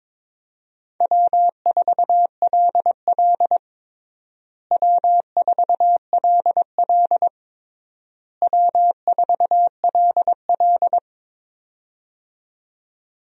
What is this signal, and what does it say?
Morse code, heard as a keyed tone, about 22 wpm, 700 Hz. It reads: W4LL W4LL W4LL
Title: USA Call Signs - 4 Characters - Repeated 3x - 22wpm